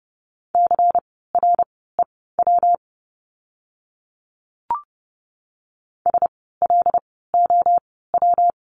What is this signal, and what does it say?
Morse code, heard as a keyed tone, about 30 wpm, spaced as for 17 wpm, 700 Hz. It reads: CREW SLOW
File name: course / 30:10wpm / 115 Lesson 19f - Letter w - Words 30wpm